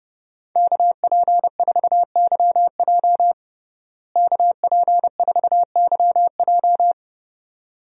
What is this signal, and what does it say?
Morse code, heard as a keyed tone, about 30 wpm, 700 Hz. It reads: KP4YJ KP4YJ
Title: DXCC-100 Call Signs - 5 Characters - Repeated 2x - 30wpm